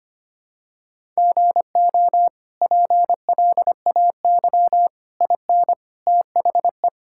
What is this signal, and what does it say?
Morse code, heard as a keyed tone, about 25 wpm, 700 Hz. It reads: GO PLAY IN THE
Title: Sentences from Top 1000 Words - Review Part 5 - 25wpm